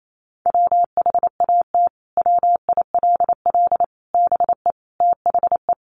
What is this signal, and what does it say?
Morse code, heard as a keyed tone, about 28 wpm, 700 Hz. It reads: WHAT WILL BE THE
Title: Sentences from Top 700 Words 28wpm